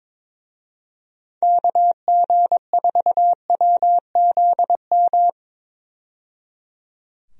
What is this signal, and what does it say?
Morse code, heard as a keyed tone, about 22 wpm, 700 Hz. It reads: KG4WZM
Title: USA Call Signs - 6 Characters - 22wpm